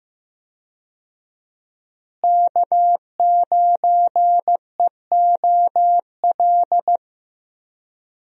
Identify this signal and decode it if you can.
Morse code, heard as a keyed tone, about 15 wpm, 700 Hz. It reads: K9EOL
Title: USA Call Signs - 5 Characters - Repeated 2x - 15wpm